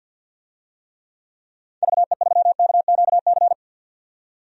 Morse code, heard as a keyed tone, about 50 wpm, 700 Hz. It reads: VE3KQC
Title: DXCC-100 Call Signs - 6 Characters - 50wpm